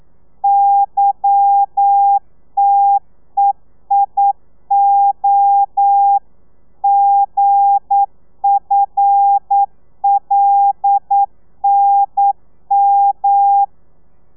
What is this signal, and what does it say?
Morse code, heard as a keyed tone, about 9 wpm, 800 Hz. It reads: YTEIO GFLNM
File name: telegrafia / i2tfj / Lez11 40